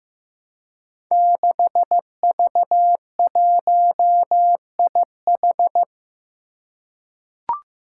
Morse code, heard as a keyed tone, about 15 wpm, 700 Hz. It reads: 6V1IH
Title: Five Challenging Letter-Number Combinations 15wpm